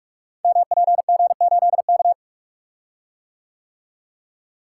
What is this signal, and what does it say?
Morse code, heard as a keyed tone, about 44 wpm, 700 Hz. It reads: MPG8K
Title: Code Groups - 5 Letter:Numbers - 45wpm